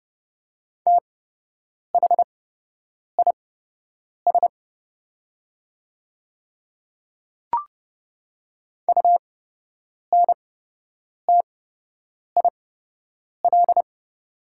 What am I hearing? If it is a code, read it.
Morse code, heard as a keyed tone, about 30 wpm, spaced as for 8 wpm, 690 Hz. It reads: THIS UNTIL
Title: 8x Character Spacing - Top 500 Words - 30wpm